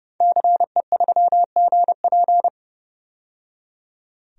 Morse code, heard as a keyed tone, about 30 wpm, 700 Hz. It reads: CE3GP
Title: DXCC-100 Call Signs - 5 Characters - 30wpm